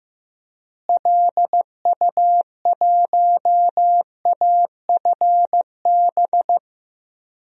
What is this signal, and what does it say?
Morse code, heard as a keyed tone, about 15 wpm, 700 Hz. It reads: LU1AFB
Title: DXCC-100 Call Signs - 6 Characters - Repeated 3x - 15wpm